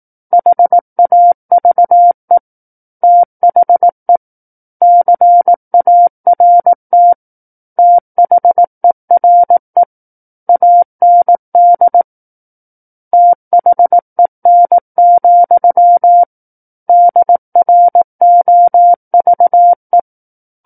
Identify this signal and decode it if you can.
Morse code, heard as a keyed tone, about 18 wpm, 700 Hz. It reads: HAVE THE CART THERE AND THEN, DROVE
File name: 18 WPM